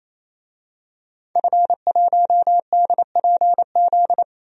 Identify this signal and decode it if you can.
Morse code, heard as a keyed tone, about 27 wpm, 700 Hz. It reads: F1DPZ
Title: DXCC-100 Call Signs - 5 Characters - Repeated 3x - 28wpm